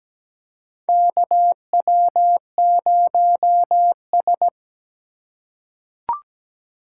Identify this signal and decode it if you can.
Morse code, heard as a keyed tone, about 17 wpm, 700 Hz. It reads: KW0S
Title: USA Call Signs - 4 Characters - 8x Character Spacing - 17wpm